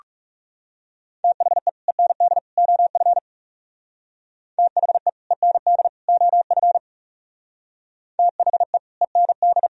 Morse code, heard as a keyed tone, about 37 wpm, 700 Hz. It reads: THE END OF THE END OF THE END
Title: Speed-Racing - Sets of 3 Words 30wpm